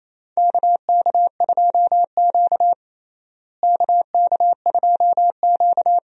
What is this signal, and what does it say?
Morse code, heard as a keyed tone, about 28 wpm, 700 Hz. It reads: KK2Q KK2Q
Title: USA Call Signs - 4 Characters - Repeated 3x - 28wpm